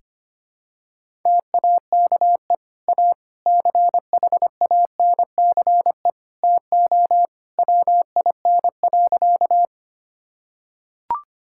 Morse code, heard as a keyed tone, about 25 wpm, 700 Hz. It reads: TAKE A CHANCE TO WIN.
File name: practice / 25wpm / Sentences with 3-Word Phrases - 25wpm